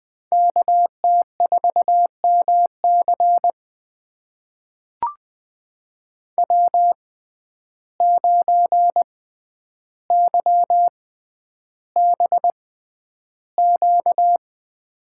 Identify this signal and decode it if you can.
Morse code, heard as a keyed tone, about 20 wpm, spaced as for 10 wpm, 700 Hz. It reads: KT4MC W9YBQ
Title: USA Call Signs - 5 Characters - 6x Character Spacing - 20wpm